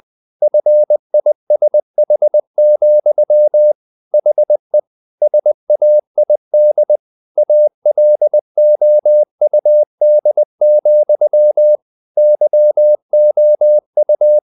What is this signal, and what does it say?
Morse code, heard as a keyed tone, about 20 wpm, 600 Hz. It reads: FISH, HE SAID ALOUD, YOU
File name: the old man and the sea / the old man and the sea part 1 20wpm